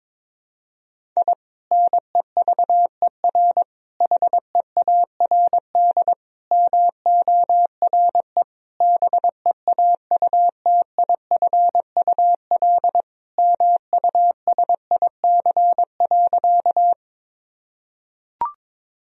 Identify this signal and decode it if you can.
Morse code, heard as a keyed tone, about 21 wpm, 700 Hz. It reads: I NEVER HEARD MORE BEAUTIFUL MUSIC.
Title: Sentences from Top 1000 Words - Review Part 1 - 22wpm